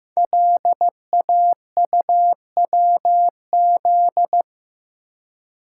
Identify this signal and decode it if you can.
Morse code, heard as a keyed tone, about 15 wpm, 700 Hz. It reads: LAUWZ
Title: Code Groups - 5 Letter:Numbers - 15wpm